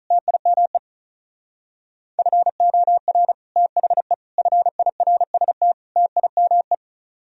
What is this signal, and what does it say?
Morse code, heard as a keyed tone, about 43 wpm, 700 Hz. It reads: TIME FOR THE FIRST TIME
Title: Speed-Racing - Sets of 4 Words 35wpm